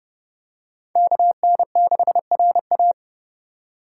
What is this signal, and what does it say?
Morse code, heard as a keyed tone, about 28 wpm, 700 Hz. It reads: KN6RA